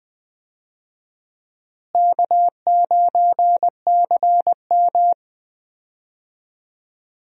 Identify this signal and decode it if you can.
Morse code, heard as a keyed tone, about 19 wpm, 700 Hz. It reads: K9CM
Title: USA Call Signs - 4 Characters - Repeated 2x - 20wpm